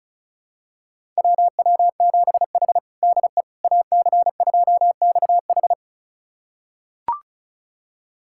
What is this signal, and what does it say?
Morse code, heard as a keyed tone, about 34 wpm, 700 Hz. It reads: WW7H DE AC2XH
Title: USA Callsign DE Callsign 35wpm